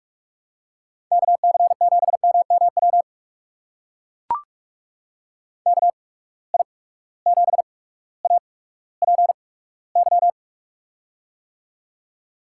Morse code, heard as a keyed tone, about 44 wpm, spaced as for 19 wpm, 700 Hz. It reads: KC7MMW KI7APY
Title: USA Call Signs - 6 Characters - 8x Character Spacing - 45wpm